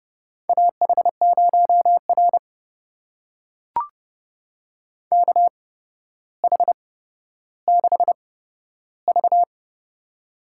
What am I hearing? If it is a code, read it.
Morse code, heard as a keyed tone, about 30 wpm, spaced as for 13 wpm, 700 Hz. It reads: AH0R KH6V
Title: USA Call Signs - 8x Character Spacing - 30wpm